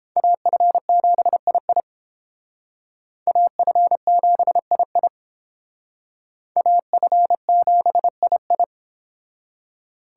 Morse code, heard as a keyed tone, about 29 wpm, 700 Hz. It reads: AF7II AF7II AF7II